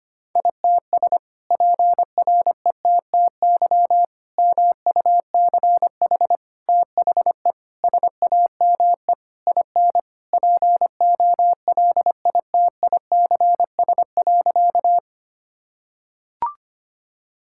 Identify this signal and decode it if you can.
Morse code, heard as a keyed tone, about 25 wpm, 700 Hz. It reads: ITS PRETTY MUCH THE SAME IN POLITICS.
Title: Sentences from Top 1000 Words - Review Part 6 - 25wpm